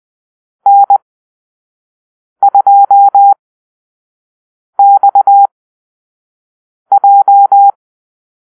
Morse code, heard as a keyed tone, about 20 wpm, spaced as for 5 wpm, 800 Hz. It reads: N2XJ
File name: arrl / Call-Sign Drill-1